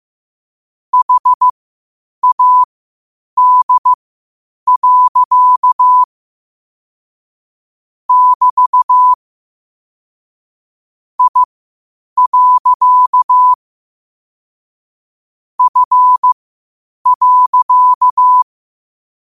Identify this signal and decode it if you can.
Morse code, heard as a keyed tone, about 15 wpm, spaced as for 8 wpm, 1000 Hz. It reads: HAD. <BT> I. F.
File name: qotdfiles / 8wpm